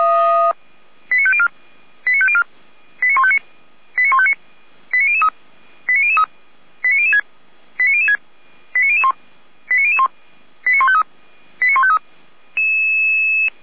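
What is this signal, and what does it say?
Five-tone selective call (ZVEI), heard as quick runs of five tones, tones of about 70 ms. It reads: 89504 89504 89159 89159 89003 89003 89006 89006 89001 89001 89153 89153
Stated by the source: ZVEI five-tone call 89160 received on an SDR, from the feuerwehr-orlamuende.org (Orlamünde fire brigade) audio server.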